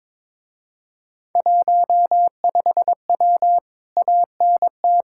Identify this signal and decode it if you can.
Morse code, heard as a keyed tone, about 22 wpm, 700 Hz. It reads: 15W ANT
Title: Speed-Racing Common QSO Phrases - 22wpm